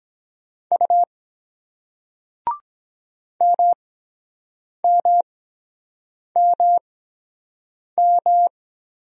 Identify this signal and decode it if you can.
Morse code, heard as a keyed tone, about 23 wpm, spaced as for 7 wpm, 700 Hz. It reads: U MMMM